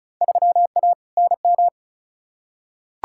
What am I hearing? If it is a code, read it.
Morse code, heard as a keyed tone, about 36 wpm, 700 Hz. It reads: 3A NM
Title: Field Day Exchanges Encore - 35wpm